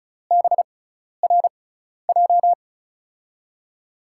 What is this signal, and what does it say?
Morse code, heard as a keyed tone, about 34 wpm, spaced as for 12 wpm, 700 Hz. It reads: BRJ
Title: USA Call Signs - 5 Characters - 6x Character Spacing - 35wpm